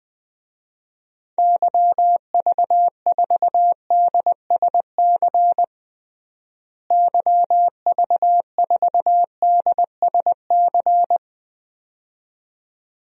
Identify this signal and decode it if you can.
Morse code, heard as a keyed tone, about 20 wpm, 700 Hz. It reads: YV4DSC YV4DSC
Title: DXCC-100 Call Signs - 6 Characters - Repeated 2x - 20wpm